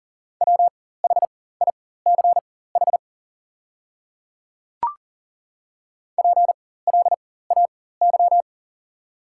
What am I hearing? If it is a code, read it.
Morse code, heard as a keyed tone, about 40 wpm, spaced as for 19 wpm, 690 Hz. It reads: WHICH PLAY
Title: 4x Character Spacing - Top 500 Words - 40wpm